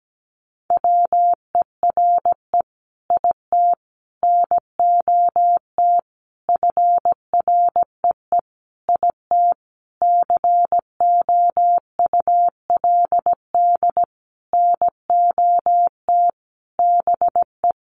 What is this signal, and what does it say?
Morse code, heard as a keyed tone, about 17 wpm, 700 Hz. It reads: WERE IT NOT FREE IT COULD NOT BE
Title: Sentences from Top 900 Words 17wpm